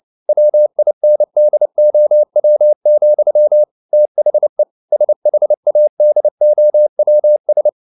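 Morse code, heard as a keyed tone, about 29 wpm, 600 Hz. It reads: WINDOW, THE SHADOWS
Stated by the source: Morse code at 29 wpm, sending Ken Follett's novel Never in English, from CW med Gnister.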